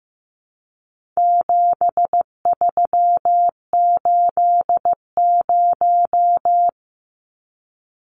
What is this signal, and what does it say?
Morse code, heard as a keyed tone, about 15 wpm, 700 Hz. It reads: 7380
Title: Numbers - 4-Digit 15wpm